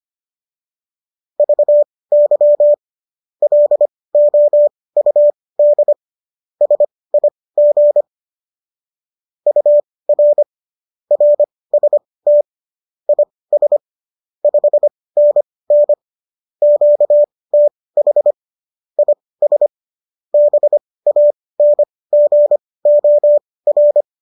Morse code, heard as a keyed tone, about 25 wpm, spaced as for 18 wpm, 590 Hz. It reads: VY LOUD SIG UR RST IS 5NN QTH IS BANGOR
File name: qso218 18